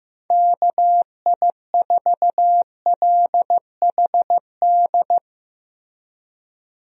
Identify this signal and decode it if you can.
Morse code, heard as a keyed tone, about 15 wpm, 700 Hz. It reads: KI4LHD
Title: USA Call Signs - 6 Characters - Repeated 2x - 15wpm